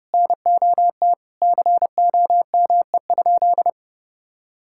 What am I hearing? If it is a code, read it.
Morse code, heard as a keyed tone, about 30 wpm, 700 Hz. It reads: NOT COME?
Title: Sentences from Top 1600 Words 30wpm